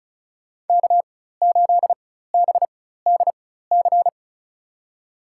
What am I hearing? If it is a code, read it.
Morse code, heard as a keyed tone, about 36 wpm, spaced as for 16 wpm, 700 Hz. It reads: K8BDC
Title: USA Call Signs - 5 Characters - 4x Character Spacing - 35wpm